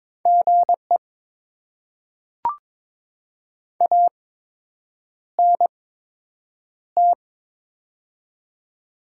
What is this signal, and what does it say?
Morse code, heard as a keyed tone, about 21 wpm, spaced as for 8 wpm, 700 Hz. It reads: GE ANT
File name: Top 20 QSO Elements - 8x Character Spacing - 22wpm